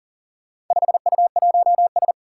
Morse code, heard as a keyed tone, about 41 wpm, 700 Hz. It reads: 5U1S
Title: Four Challenging Letter-Number Combinations 40wpm